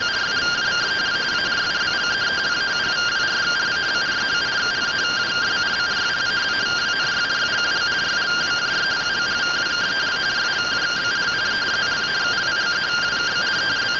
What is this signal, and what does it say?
Two-tone teleprinter signal, about 170 Hz shift.